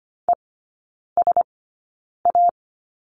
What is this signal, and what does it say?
Morse code, heard as a keyed tone, about 25 wpm, spaced as for 9 wpm, 700 Hz. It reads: ESA
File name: course / 25:10wpm / 093 Lesson 16e - Letter c - Three Characters 25wpm